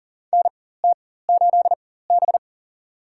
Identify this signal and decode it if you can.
Morse code, heard as a keyed tone, about 40 wpm, spaced as for 19 wpm, 700 Hz. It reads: NT8B
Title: USA Call Signs - 4 Characters - 4x Character Spacing - 40wpm